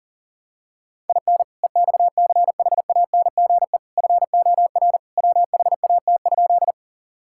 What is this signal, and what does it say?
Morse code, heard as a keyed tone, about 40 wpm, 700 Hz. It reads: IN EXCHANGE FOR WHAT?
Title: Sentences from Top 1000 Words - Review Part 6 - 40wpm